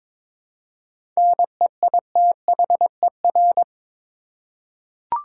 Morse code, heard as a keyed tone, about 22 wpm, 700 Hz. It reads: NEITHER
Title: Seven Letter Words - 22wpm